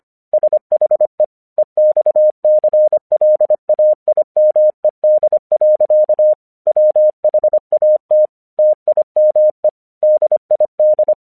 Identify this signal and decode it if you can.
Morse code, heard as a keyed tone, about 26 wpm, 610 Hz. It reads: SHE EXCLAIMED. WHAT TIME DID
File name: 1 English 25wpm / AQC34